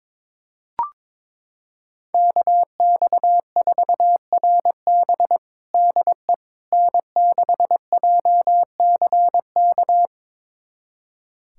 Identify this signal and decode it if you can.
Morse code, heard as a keyed tone, about 22 wpm, 700 Hz. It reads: KX4RB DE N6JCK
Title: USA Callsign DE Callsign - 5 Characters - 22wpm